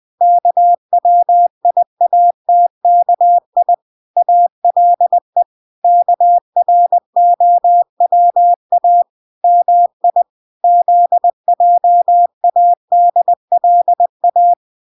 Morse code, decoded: KWIATKI ALE KROWA MI ZJADLA

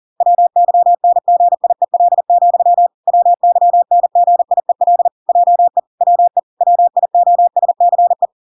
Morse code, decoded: WYNGIEL, WYNGIEL JE WE WIOSCE

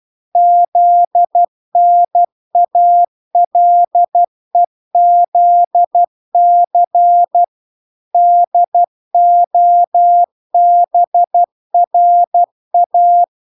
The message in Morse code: ZNALEZC DOBRA

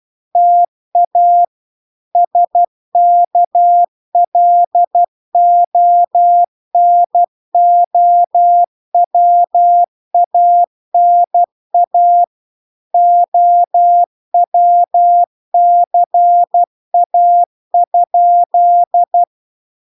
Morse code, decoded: TA SKLONOWANA OWCA?

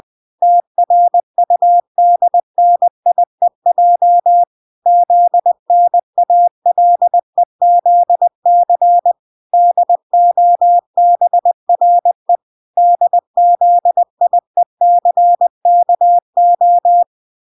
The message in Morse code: TRUDNIEJ ZNALEZC DOBRE DZIECKO

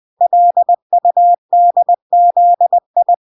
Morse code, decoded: LUDZI

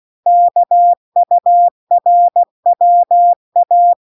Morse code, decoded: KURWA

